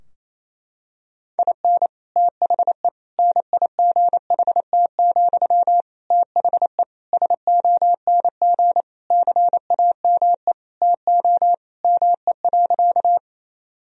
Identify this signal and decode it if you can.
Morse code, heard as a keyed tone, about 28 wpm, 700 Hz. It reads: IN THE NIGHT, THE SONG CAME TO ME.
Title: Sentences from Top 1000 Words - Review Part 2 - 28wpm